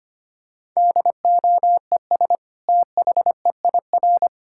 Morse code, heard as a keyed tone, about 25 wpm, 700 Hz. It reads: DOES THEIR
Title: Sentences from Top 1000 Words - Review Part 5 - 25wpm